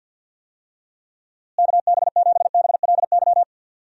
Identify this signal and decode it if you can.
Morse code, heard as a keyed tone, about 52 wpm, 700 Hz. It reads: KB7BLY
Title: USA Call Signs - 6 Characters - Repeated 3x - 50wpm